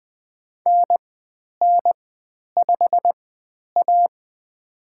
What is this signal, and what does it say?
Morse code, heard as a keyed tone, about 20 wpm, spaced as for 10 wpm, 700 Hz. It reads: NN5A
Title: 095 Lesson 16g - Letter c - Call Signs 20wpm